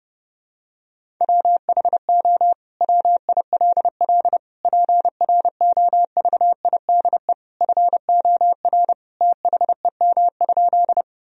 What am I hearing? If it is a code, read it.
Morse code, heard as a keyed tone, about 30 wpm, 700 Hz. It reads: WHO WILL PROVIDE FOR THEM?
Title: Sentences from Top 1000 Words - Review Part 3 - 30wpm